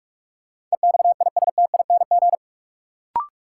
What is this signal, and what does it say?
Morse code, heard as a keyed tone, about 42 wpm, 700 Hz. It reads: EXISTING
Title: Sentences from Top 1500 Words - New Word Drill Alt - 45wpm